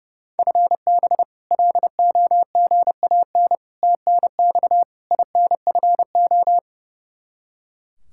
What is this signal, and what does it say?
Morse code, heard as a keyed tone, about 32 wpm, 700 Hz. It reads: FB LOGAN TNX INFO
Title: K1USN SST QSO Practice 30wpm